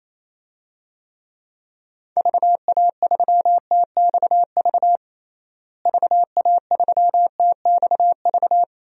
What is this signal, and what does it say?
Morse code, heard as a keyed tone, about 28 wpm, 700 Hz. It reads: VA3TXV VA3TXV
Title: DXCC-100 Call Signs - 6 Characters - Repeated 2x - 28wpm